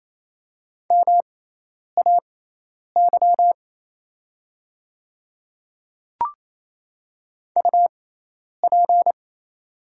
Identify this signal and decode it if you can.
Morse code, heard as a keyed tone, about 27 wpm, spaced as for 10 wpm, 700 Hz. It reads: MAY UP
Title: 6x Character Spacing - Top 100 Words - 28wpm